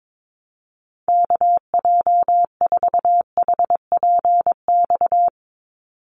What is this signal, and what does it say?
Morse code, heard as a keyed tone, about 22 wpm, 700 Hz. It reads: KJ4HPX
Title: USA Call Signs 22wpm